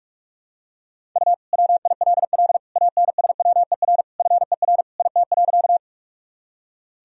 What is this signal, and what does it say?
Morse code, heard as a keyed tone, about 46 wpm, 700 Hz. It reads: U WILL ANSWER FER IT.
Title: Sentences from Top 300 Words and Abbreviations 45wpm